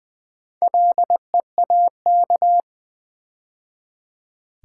Morse code, heard as a keyed tone, about 20 wpm, 700 Hz. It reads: LEAK
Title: Build A Word - 2 Letters At A Time - 4-Letter Words - 20wpm